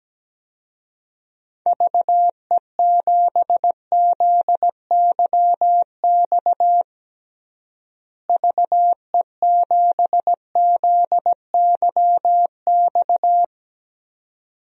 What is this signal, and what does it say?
Morse code, heard as a keyed tone, about 17 wpm, 700 Hz. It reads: VE7ZYX VE7ZYX
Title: DXCC-100 Call Signs - 6 Characters - Repeated 2x - 17wpm